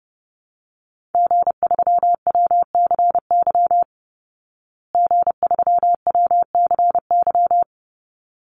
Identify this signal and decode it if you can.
Morse code, heard as a keyed tone, about 30 wpm, 700 Hz. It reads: G3WCY G3WCY
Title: Call Signs Repeated 3x - 30wpm